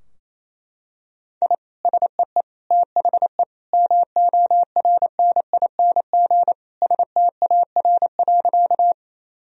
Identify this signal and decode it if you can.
Morse code, heard as a keyed tone, about 28 wpm, 700 Hz. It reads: I SEE THE MORNING STAR.
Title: Sentences from Top 1000 Words - Review Part 2 - 28wpm